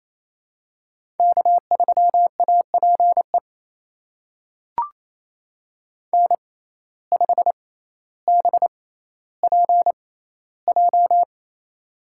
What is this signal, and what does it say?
Morse code, heard as a keyed tone, about 28 wpm, spaced as for 14 wpm, 700 Hz. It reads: K3APE N5BPJ
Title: USA Call Signs - 5 Characters - 6x Character Spacing - 28wpm